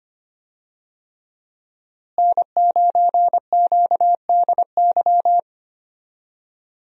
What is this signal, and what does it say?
Morse code, heard as a keyed tone, about 25 wpm, 700 Hz. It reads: N9QDY
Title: USA Call Signs - 5 Characters - Repeated 3x - 25wpm